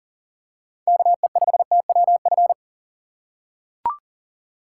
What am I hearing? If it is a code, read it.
Morse code, heard as a keyed tone, about 40 wpm, 700 Hz. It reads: KE5TWF
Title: USA Call Signs - 6 Characters - 4x Character Spacing - 40wpm